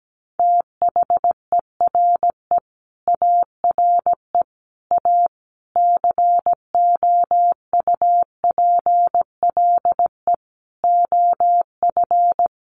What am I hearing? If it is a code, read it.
Morse code, heard as a keyed tone, about 17 wpm, 700 Hz. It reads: THERE ARE A COUPLE OF
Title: Sets of 5 Words Encore 17wpm